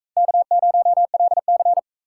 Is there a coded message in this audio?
Yes: Morse code from 0.2 to 1.8 s, heard as a keyed tone, about 41 wpm, 700 Hz. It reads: K0LC